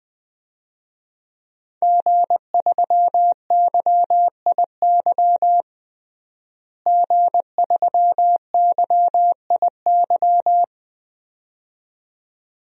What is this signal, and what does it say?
Morse code, heard as a keyed tone, about 21 wpm, 700 Hz. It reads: G3YIY G3YIY
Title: DXCC-100 Call Signs - 5 Characters - Repeated 2x - 20wpm